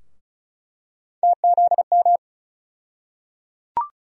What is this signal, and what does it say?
Morse code, heard as a keyed tone, about 36 wpm, 700 Hz. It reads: TZM
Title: USA Call Sign Postfixes - 3 Letters 35wpm